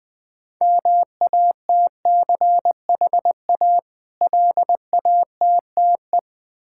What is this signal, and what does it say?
Morse code, heard as a keyed tone, about 20 wpm, 700 Hz. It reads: MATCHA LATTE
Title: Difficult Words with Hints - 20wpm